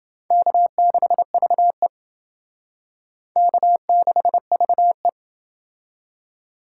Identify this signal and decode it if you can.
Morse code, heard as a keyed tone, about 28 wpm, 700 Hz. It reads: K6VE K6VE